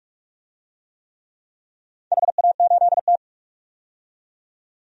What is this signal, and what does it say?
Morse code, heard as a keyed tone, about 44 wpm, 700 Hz. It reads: HA8T
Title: DXCC-100 Call Signs - 4 Characters - 45wpm